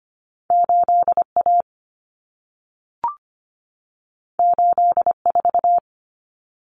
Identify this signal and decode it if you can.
Morse code, heard as a keyed tone, about 25 wpm, 700 Hz. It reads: 8A 84
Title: Common Cut Numbers - 2-Digit 25wpm